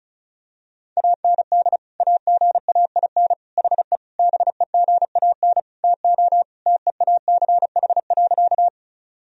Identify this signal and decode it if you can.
Morse code, heard as a keyed tone, about 36 wpm, 700 Hz. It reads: AND AGAIN HE BEGAN TO TEACH.
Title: Sentences from Top 1000 Words - Review Part 2 - 35wpm